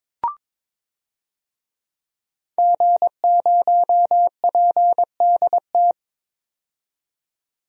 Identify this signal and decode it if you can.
Morse code, heard as a keyed tone, about 22 wpm, 700 Hz. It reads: G0PDT
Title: DXCC-100 Call Signs - 5 Characters - 22wpm